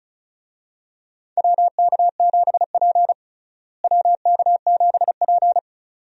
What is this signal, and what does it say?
Morse code, heard as a keyed tone, about 36 wpm, 700 Hz. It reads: WK7P WK7P
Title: USA Call Signs - 4 Characters - Repeated 2x - 35wpm